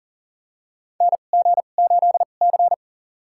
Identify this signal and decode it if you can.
Morse code, heard as a keyed tone, about 40 wpm, spaced as for 29 wpm, 700 Hz. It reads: NG8C
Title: USA Call Signs - 4 Characters - 2x Character Spacing - 40wpm